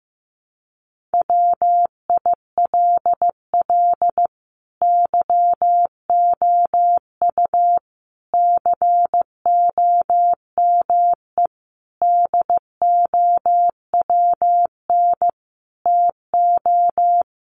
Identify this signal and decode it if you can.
Morse code, heard as a keyed tone, about 15 wpm, 700 Hz. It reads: WILL YOU COME DOWN TO
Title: Sentences from Top 1500 Words 15wpm